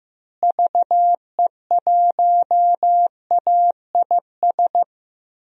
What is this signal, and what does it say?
Morse code, heard as a keyed tone, about 15 wpm, 700 Hz. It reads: VE1AIS
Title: DXCC-100 Call Signs - 6 Characters - Repeated 2x - 15wpm